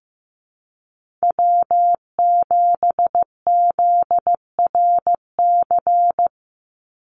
Morse code, heard as a keyed tone, about 15 wpm, 700 Hz. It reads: W7ZRC
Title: Call Signs Repeated 3x - 15wpm